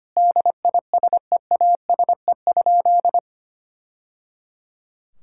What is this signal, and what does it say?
Morse code, heard as a keyed tone, about 25 wpm, 700 Hz. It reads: DISEASE?
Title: Sentences from Top 1000 Words - Review Part 6 - 25wpm